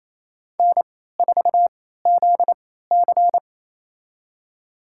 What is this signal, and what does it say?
Morse code, heard as a keyed tone, about 28 wpm, spaced as for 16 wpm, 700 Hz. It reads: N4ZC
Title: USA Call Signs - 4 Characters - 3x Character Spacing - 28wpm